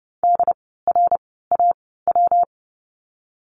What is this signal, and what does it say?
Morse code, heard as a keyed tone, about 30 wpm, spaced as for 17 wpm, 700 Hz. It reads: DRAW
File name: course / 30:10wpm / 115 Lesson 19f - Letter w - Words 30wpm